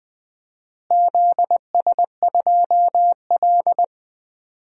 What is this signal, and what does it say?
Morse code, heard as a keyed tone, about 20 wpm, 700 Hz. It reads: ZS2L
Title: DXCC-100 Call Signs - 4 Characters - Repeated 2x - 20wpm